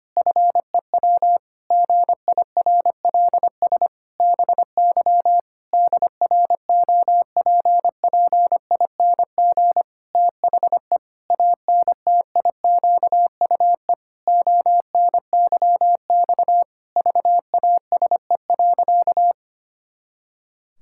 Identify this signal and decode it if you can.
Morse code, heard as a keyed tone, about 25 wpm, 700 Hz. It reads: FEW GIRLS BY DROPPING THE ANTIQUE ONYX VASE.
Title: Pangram Challenge v2 - 25wpm